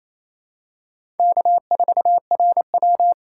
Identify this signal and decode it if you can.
Morse code, heard as a keyed tone, about 28 wpm, 700 Hz. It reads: K4RW
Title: USA Call Signs - 4 Characters - 4x Character Spacing - 28wpm